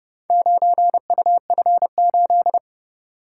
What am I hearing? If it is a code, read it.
Morse code, heard as a keyed tone, about 29 wpm, 700 Hz. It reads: 9UF8